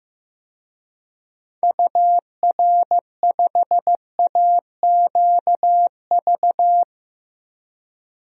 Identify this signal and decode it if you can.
Morse code, heard as a keyed tone, about 15 wpm, 700 Hz. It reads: UR5AQV
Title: DXCC-100 Call Signs - 6 Characters - Repeated 2x - 15wpm